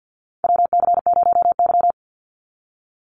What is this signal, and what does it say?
Morse code, heard as a keyed tone, about 50 wpm, 700 Hz. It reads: WC0Y
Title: NA SOTA Chaser Callsigns 50wpm